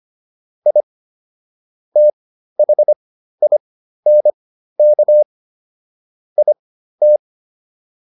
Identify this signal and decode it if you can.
Morse code, heard as a keyed tone, about 25 wpm, spaced as for 13 wpm, 600 Hz. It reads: I THINK IT